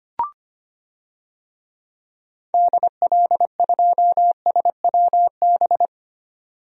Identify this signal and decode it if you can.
Morse code, heard as a keyed tone, about 25 wpm, 700 Hz. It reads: DL2SWB
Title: DXCC-100 Call Signs - 6 Characters - 25wpm